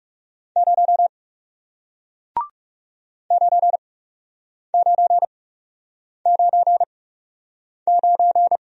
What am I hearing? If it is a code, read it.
Morse code, heard as a keyed tone, about 38 wpm, spaced as for 8 wpm, 700 Hz. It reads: 0 9999